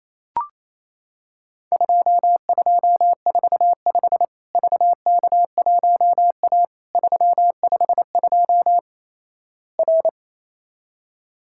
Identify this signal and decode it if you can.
Morse code, heard as a keyed tone, about 28 wpm, 700 Hz. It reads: 2245 VK1A 352 R
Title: Worked All Europe DX Contest - QTC Exchange - 28wpm